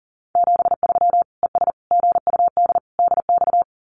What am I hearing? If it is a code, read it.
Morse code, heard as a keyed tone, about 40 wpm, 700 Hz. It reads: 73 ES GUD DX
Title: Common QSO Phrases 40wpm